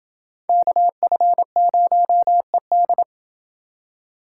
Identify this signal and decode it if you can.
Morse code, heard as a keyed tone, about 28 wpm, 700 Hz. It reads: KF0ED